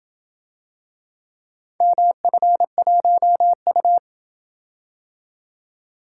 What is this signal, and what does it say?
Morse code, heard as a keyed tone, about 27 wpm, 700 Hz. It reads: MF1U